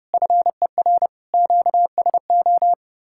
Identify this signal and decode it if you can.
Morse code, heard as a keyed tone, about 37 wpm, 700 Hz. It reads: FER QSO
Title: Speed-Racing Common QSO Phrases Encore - 30wpm